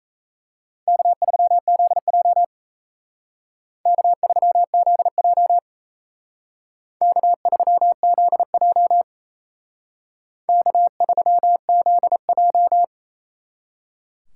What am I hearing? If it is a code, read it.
Morse code, heard as a keyed tone, about 36 wpm, 700 Hz. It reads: K3ZJ K3ZJ K3ZJ K3ZJ